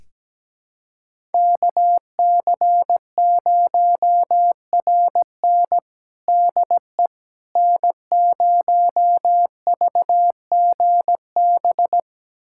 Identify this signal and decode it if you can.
Morse code, heard as a keyed tone, about 17 wpm, 700 Hz. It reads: KC0RN DE N0VGB
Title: USA Callsign DE Callsign 17wpm